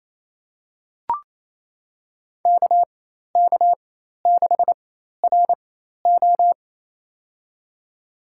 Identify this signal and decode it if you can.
Morse code, heard as a keyed tone, about 28 wpm, spaced as for 13 wpm, 700 Hz. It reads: KK6RO
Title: USA Call Signs - 5 Characters - 4x Character Spacing - 28wpm